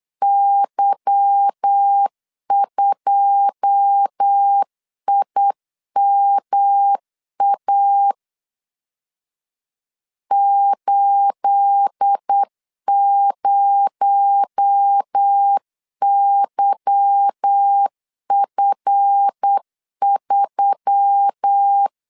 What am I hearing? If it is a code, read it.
Morse code, heard as a keyed tone, about 8 wpm, 800 Hz. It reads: Y2IMA 80YF3
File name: LEZIONI / 9 28 8wpm 3512